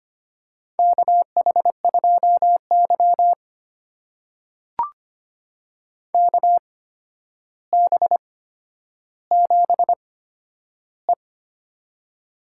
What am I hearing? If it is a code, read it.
Morse code, heard as a keyed tone, about 25 wpm, spaced as for 11 wpm, 700 Hz. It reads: KH2Y KB7E
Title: USA Call Signs - 4 Characters - 8x Character Spacing - 25wpm